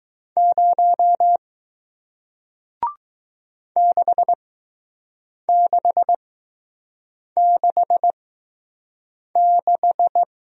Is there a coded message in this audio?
Yes: Morse code from 0.4 to 10.2 s, heard as a keyed tone, about 20 wpm, spaced as for 6 wpm, 700 Hz. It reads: 0 6666